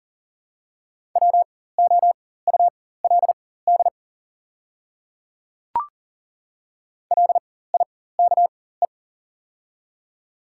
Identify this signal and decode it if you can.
Morse code, heard as a keyed tone, about 40 wpm, spaced as for 19 wpm, 700 Hz. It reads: WOULD LIKE
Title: 4x Character Spacing - Top 100 Words - 40wpm